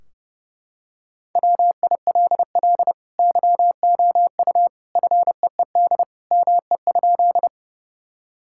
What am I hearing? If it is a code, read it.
Morse code, heard as a keyed tone, about 30 wpm, 700 Hz. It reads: WILL YOU FEED ME?
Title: Sentences from Top 1600 Words 30wpm